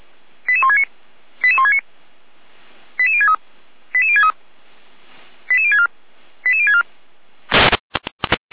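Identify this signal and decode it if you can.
Five-tone selective call (ZVEI), heard as quick runs of five tones, tones of about 70 ms. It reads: 88169 88169 88063 88063 88064 88064
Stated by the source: ZVEI five-tone selective call 88365, received by SDR.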